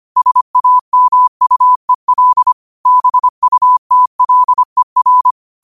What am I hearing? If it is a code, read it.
Morse code, heard as a keyed tone, about 25 wpm, 1000 Hz. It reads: SAMUEL BUTLER
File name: qotdfiles / 25wpm